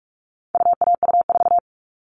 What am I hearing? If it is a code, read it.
Morse code, heard as a keyed tone, about 44 wpm, 700 Hz. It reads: UAU4
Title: All Cut Numbers - 4-Digit 45wpm